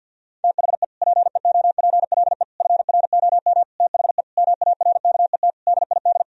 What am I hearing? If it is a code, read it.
Morse code, heard as a keyed tone, about 50 wpm, 700 Hz. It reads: THE PEOPLE FROM THE MARKET DID
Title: Sentences from Top 1000 Words - Review Part 3 - 50wpm